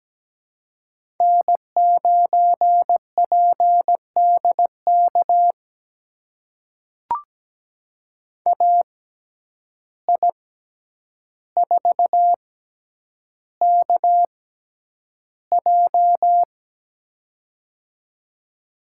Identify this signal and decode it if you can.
Morse code, heard as a keyed tone, about 17 wpm, spaced as for 9 wpm, 700 Hz. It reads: N9PDK AI4KJ
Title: USA Call Signs - 5 Characters - 6x Character Spacing - 17wpm